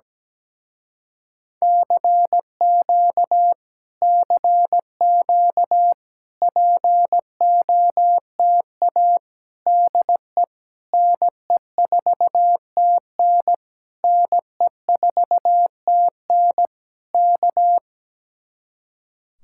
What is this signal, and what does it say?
Morse code, heard as a keyed tone, about 17 wpm, 700 Hz. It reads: CQ CQ POTA DE NE4TN NE4TN K
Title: Speed-Racing Common QSO Phrases Encore - 17wpm